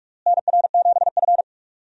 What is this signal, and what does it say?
Morse code, heard as a keyed tone, about 43 wpm, 700 Hz. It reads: NR7F
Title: USA Call Signs - 4 Characters - Repeated 2x - 45wpm